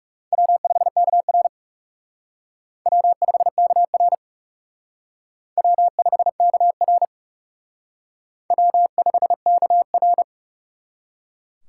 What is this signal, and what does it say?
Morse code, heard as a keyed tone, about 38 wpm, 700 Hz. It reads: W5KR W5KR W5KR W5KR